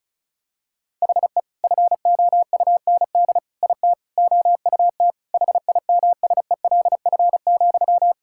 Morse code, heard as a keyed tone, about 34 wpm, 700 Hz. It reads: HE FOUND IT OUT HIMSELF,
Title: Sentences from Top 1600 Words 35wpm